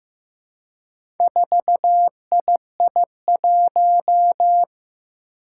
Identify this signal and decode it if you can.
Morse code, heard as a keyed tone, about 15 wpm, 700 Hz. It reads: 4II1
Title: Four Challenging Letter-Number Combinations 15wpm